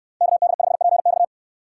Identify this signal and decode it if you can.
Morse code, heard as a keyed tone, about 69 wpm, 700 Hz. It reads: KD5PX